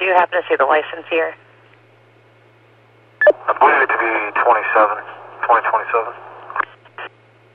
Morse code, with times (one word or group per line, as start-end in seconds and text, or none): none